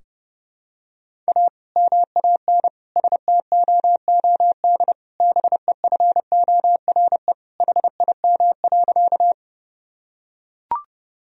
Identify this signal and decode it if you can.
Morse code, heard as a keyed tone, about 30 wpm, 700 Hz. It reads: A MAN STOOD BEFORE HIM.
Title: Sentences from Top 1000 Words - Review Part 2 - 30wpm